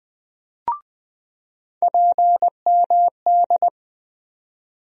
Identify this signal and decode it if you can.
Morse code, heard as a keyed tone, about 20 wpm, 700 Hz. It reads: PMD
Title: USA Call Sign Postfixes - 3 Letters 20wpm